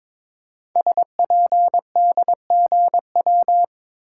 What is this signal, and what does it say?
Morse code, heard as a keyed tone, about 22 wpm, 700 Hz. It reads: SPDGW
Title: Code Groups - 5 Letters - 22wpm